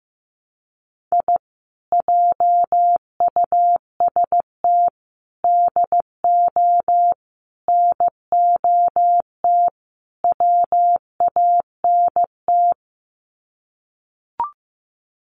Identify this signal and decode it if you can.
Morse code, heard as a keyed tone, about 15 wpm, 700 Hz. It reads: I JUST DO NOT WANT
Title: Sets of 5 Words Encore2 15wpm